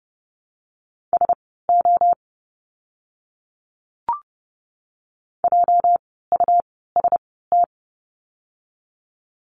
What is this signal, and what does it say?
Morse code, heard as a keyed tone, about 30 wpm, spaced as for 17 wpm, 700 Hz. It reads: SO JUST
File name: course / 30:10wpm / 230 Lesson 37f - Letter x - Words 30wpm